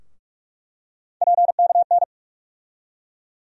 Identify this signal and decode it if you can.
Morse code, heard as a keyed tone, about 46 wpm, 700 Hz. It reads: PKN
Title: USA Call Sign Postfixes - 3 Letters 45wpm